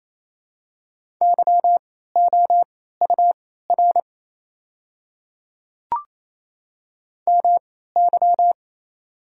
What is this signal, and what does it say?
Morse code, heard as a keyed tone, about 28 wpm, spaced as for 16 wpm, 700 Hz. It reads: YOUR MY
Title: 3x Character Spacing - Top 100 Words - 28wpm